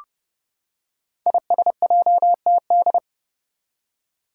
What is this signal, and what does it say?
Morse code, heard as a keyed tone, about 30 wpm, 700 Hz. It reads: ISJTD
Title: Five Challenging Letter-Number Combinations 30wpm